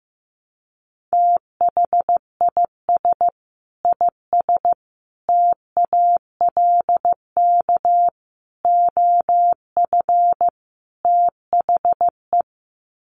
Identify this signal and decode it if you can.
Morse code, heard as a keyed tone, about 15 wpm, 700 Hz. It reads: THIS IS TALK OF THE
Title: Sets of 5 Words 15wpm